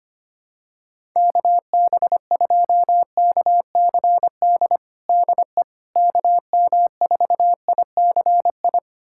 Morse code, heard as a keyed tone, about 25 wpm, 700 Hz. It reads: KB2KCD DE KM4ICI
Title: USA Callsign DE Callsign 25wpm